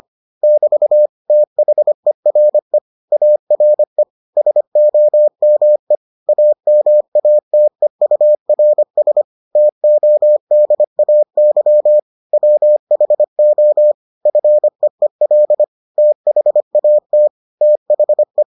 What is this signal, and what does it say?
Morse code, heard as a keyed tone, about 25 wpm, 600 Hz. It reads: <BT> THERE ARE SOME AMATEURS TODAY WHO FEEL THAT THE